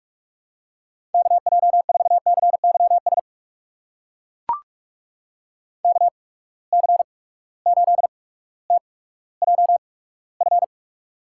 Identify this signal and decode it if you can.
Morse code, heard as a keyed tone, about 42 wpm, spaced as for 19 wpm, 700 Hz. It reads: KJ4CYS KC8TJF
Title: USA Call Signs - 6 Characters - 8x Character Spacing - 45wpm